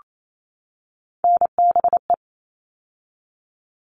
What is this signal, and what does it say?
Morse code, heard as a keyed tone, about 28 wpm, 700 Hz. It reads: NBE
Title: All Cut Numbers - 3-Digit 28wpm